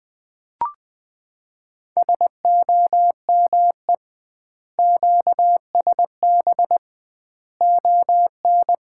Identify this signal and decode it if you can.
Morse code, heard as a keyed tone, about 20 wpm, 700 Hz. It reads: SOME QSB ON
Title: Common QSO Phrases - 2x Word-Spacing - 20wpm